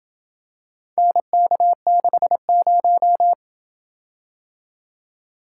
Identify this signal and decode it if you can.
Morse code, heard as a keyed tone, about 27 wpm, 700 Hz. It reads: NK60